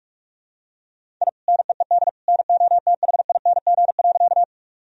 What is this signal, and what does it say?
Morse code, heard as a keyed tone, about 46 wpm, 700 Hz. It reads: I NEED NOTHING.
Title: Sentences from Top 1000 Words - Review Part 1 - 45wpm